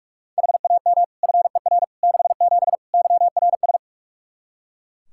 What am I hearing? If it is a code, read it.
Morse code, heard as a keyed tone, about 42 wpm, 700 Hz. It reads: HAM FER 67 YRS
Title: Practice QSO - Exchange 3 - More Info - 45wpm